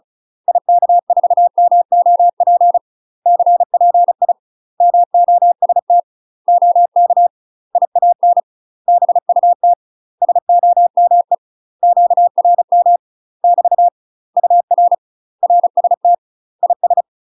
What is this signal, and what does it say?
Morse code, heard as a keyed tone, about 34 wpm, 700 Hz. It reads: IK4MOP CPI MOST OK IAN BUT SOME QRM <BT> UR RST IS
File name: UK MASTER QSOs TRAINING / 35WPM / UK MASTER QSOs 0043